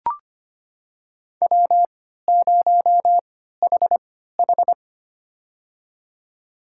Morse code, heard as a keyed tone, about 25 wpm, spaced as for 14 wpm, 700 Hz. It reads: W0HH
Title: USA Call Signs - 4 Characters - 3x Character Spacing - 25wpm